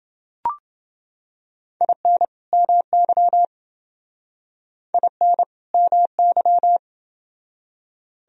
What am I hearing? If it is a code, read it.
Morse code, heard as a keyed tone, about 29 wpm, 700 Hz. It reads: IN MY IN MY